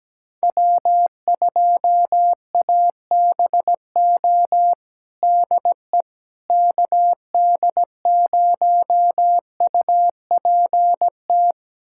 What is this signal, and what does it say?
Morse code, heard as a keyed tone, about 17 wpm, 700 Hz. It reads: W2ABO DE KD0UPT
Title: USA Callsign DE Callsign 17wpm